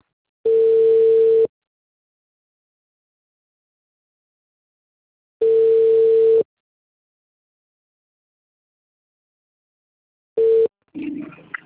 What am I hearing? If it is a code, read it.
Ringback tone (China, 450 Hz).